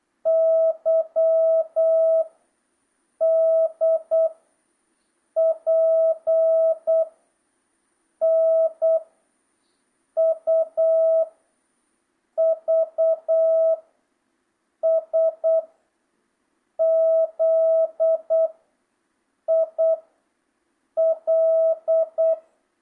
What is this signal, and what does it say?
Morse code, heard as a keyed tone, about 8 wpm, 650 Hz. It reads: Y D P N U V S Z I L
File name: jh8cbh / 2025cw